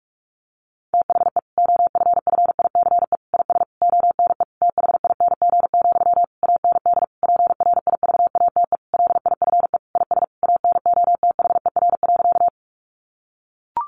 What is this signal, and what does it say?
Morse code, heard as a keyed tone, about 44 wpm, 700 Hz. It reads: THE OFFICE IS ONE THING, AND PRIVATE LIFE IS ANOTHER.
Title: Sentences from Top 1000 Words 45wpm